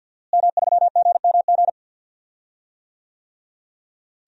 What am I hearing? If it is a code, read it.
Morse code, heard as a keyed tone, about 48 wpm, 700 Hz. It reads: M3GMG
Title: DXCC-100 Call Signs - 5 Characters - Repeated 3x - 50wpm